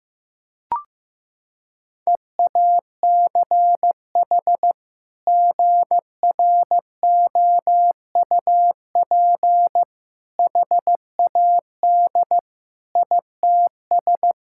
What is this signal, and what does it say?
Morse code, heard as a keyed tone, about 15 wpm, 700 Hz. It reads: EACH GROUP HAD ITS
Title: Sentences from Top 1000 Words - Review Part 6 - 15wpm